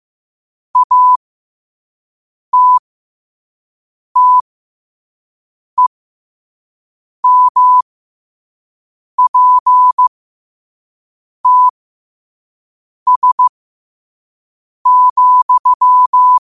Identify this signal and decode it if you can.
Morse code, heard as a keyed tone, about 15 wpm, spaced as for 5 wpm, 1000 Hz. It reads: ATTEMPTS,
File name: qotdfiles / 5wpm